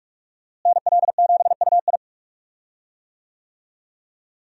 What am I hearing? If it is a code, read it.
Morse code, heard as a keyed tone, about 41 wpm, 700 Hz. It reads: NL7UI
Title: USA Call Signs - 5 Characters - Repeated 2x - 45wpm